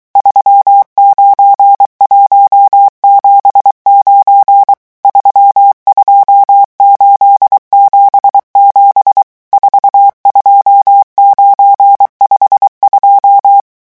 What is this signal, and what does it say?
Morse code, heard as a keyed tone, about 23 wpm, 800 Hz. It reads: 39179 32877 42952
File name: morse mp3 / revision11 chiffres vitesse 22